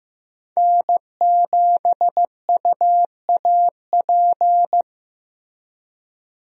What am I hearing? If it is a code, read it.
Morse code, heard as a keyed tone, about 16 wpm, 700 Hz. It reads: N7UAP
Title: USA Call Signs - 5 Characters - Repeated 3x - 15wpm